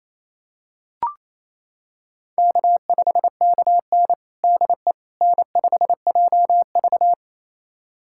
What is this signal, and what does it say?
Morse code, heard as a keyed tone, about 28 wpm, 700 Hz. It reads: K5KN DE N5JV
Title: USA Callsign DE Callsign - 4 Characters - 28wpm